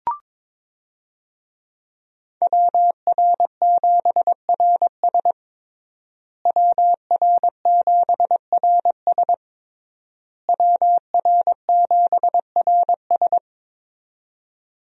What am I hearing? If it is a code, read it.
Morse code, heard as a keyed tone, about 22 wpm, 700 Hz. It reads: WR7RS WR7RS WR7RS
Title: USA Call Signs - 5 Characters - Repeated 3x - 22wpm